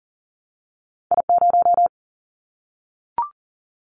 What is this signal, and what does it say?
Morse code, heard as a keyed tone, about 40 wpm, 700 Hz. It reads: I0
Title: Callsign Prefixes for Top 25 DXCC - Repeated 3x - 40wpm